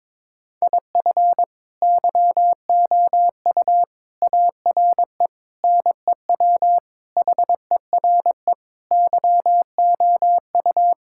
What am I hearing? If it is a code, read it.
Morse code, heard as a keyed tone, about 22 wpm, 700 Hz. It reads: IF YOU ARE NEW HERE YOU
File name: practice / 22wpm / Sentences with 8-Word Phrases - 22wpm